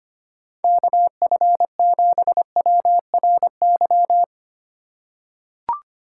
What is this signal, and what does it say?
Morse code, heard as a keyed tone, about 25 wpm, 700 Hz. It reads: KF7WRY
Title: USA Call Signs - 3x Character Spacing - 25wpm